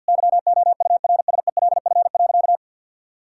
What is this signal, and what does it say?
Morse code, heard as a keyed tone, about 51 wpm, 700 Hz. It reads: YOURSELF.
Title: Sentences from Top 1600 Words 50wpm